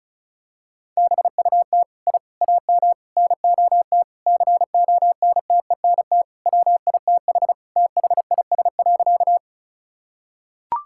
Morse code, heard as a keyed tone, about 34 wpm, 700 Hz. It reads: BUT I AM NOT CONTENT WITH THIS.
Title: Sentences from Top 1000 Words - Review Part 6 - 35wpm